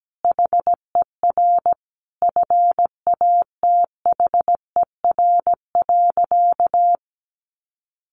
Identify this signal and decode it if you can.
Morse code, heard as a keyed tone, about 17 wpm, 700 Hz. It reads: HER FATHER.
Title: Sentences from Top 1500 Words 17wpm